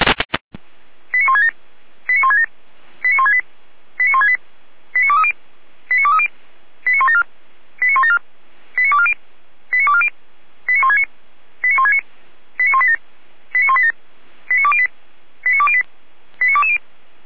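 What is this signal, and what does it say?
Five-tone selective call (ZVEI), heard as quick runs of five tones, tones of about 70 ms. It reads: 89167 89167 89168 89168 89230 89230 89174 89174 89240 89240 89169 89169 89187 89187 89208 89208 89220
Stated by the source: ZVEI five-tone call received by SDR, from server of Feuerwehr Orlamünde fire brigade.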